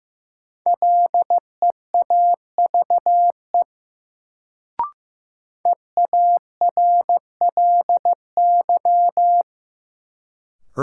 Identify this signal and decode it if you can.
Morse code, heard as a keyed tone, about 15 wpm, 700 Hz. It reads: LEAVE EARLY
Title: Sentences from Top 400 Words - New Word Drill - 15wpm